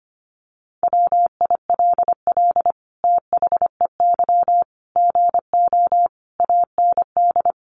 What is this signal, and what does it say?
Morse code, heard as a keyed tone, about 25 wpm, 700 Hz. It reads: WILL THEY GO AND
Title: Sentences from Top 1100 Words 25wpm